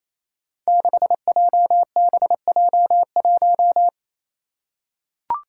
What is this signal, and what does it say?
Morse code, heard as a keyed tone, about 27 wpm, 700 Hz. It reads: 6JBJ1
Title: Five Challenging Letter-Number Combinations 28wpm